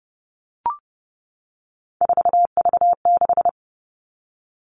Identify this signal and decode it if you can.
Morse code, heard as a keyed tone, about 30 wpm, 700 Hz. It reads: 4V6
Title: All Cut Numbers - 3-Digit 30wpm